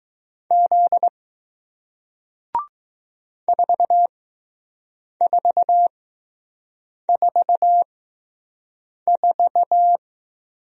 Morse code, heard as a keyed tone, about 20 wpm, spaced as for 6 wpm, 700 Hz. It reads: Z 4444